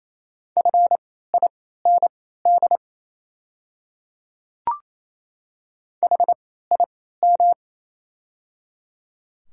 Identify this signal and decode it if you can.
Morse code, heard as a keyed tone, about 28 wpm, spaced as for 16 wpm, 700 Hz. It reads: FIND HIM
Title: 3x Character Spacing - Top 500 Words - 28wpm